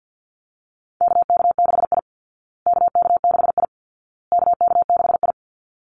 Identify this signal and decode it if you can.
Morse code, heard as a keyed tone, about 50 wpm, 700 Hz. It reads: KK6I KK6I KK6I
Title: Call Signs Repeated 2x - 50wpm